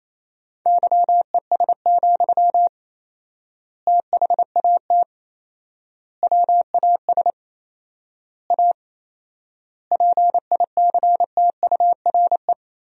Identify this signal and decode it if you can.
Morse code, heard as a keyed tone, about 28 wpm, 700 Hz. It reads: YES, THAT WAS A PICTURE
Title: 4x Word-Spacing - Sentences from Top 500 Words - 28wpm